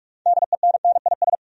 Morse code, heard as a keyed tone, about 46 wpm, 700 Hz. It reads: DENNIS
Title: USA Common Names - 45wpm